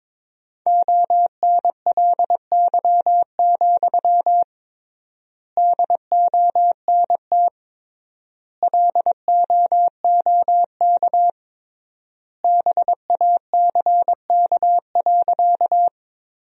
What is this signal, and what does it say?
Morse code, heard as a keyed tone, about 22 wpm, 700 Hz. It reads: ONLY, DONT LOOK BACK.
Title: 3x Word-Spacing - Sentences from Top 300 Words - 22wpm